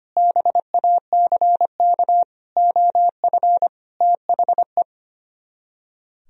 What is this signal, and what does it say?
Morse code, heard as a keyed tone, about 32 wpm, 700 Hz. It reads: BACK OF THE